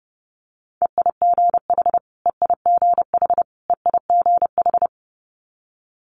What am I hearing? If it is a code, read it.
Morse code, heard as a keyed tone, about 30 wpm, 690 Hz. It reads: EIGH EIGH EIGH
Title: Multi-letter Phonemes Repeated 3x 30wpm